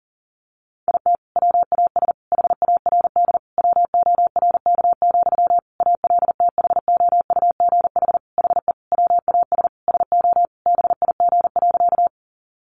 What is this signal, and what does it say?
Morse code, heard as a keyed tone, about 40 wpm, 700 Hz. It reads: IT WAS HARD WORK, ALTHOUGH HE WAS SO BIG.
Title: Sentences from Top 600 Words 40wpm